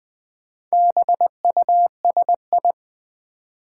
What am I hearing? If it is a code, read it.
Morse code, heard as a keyed tone, about 20 wpm, 700 Hz. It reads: BUSI
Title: Build A Word - 4 Letters At A Time - 8-Letter Words - 20wpm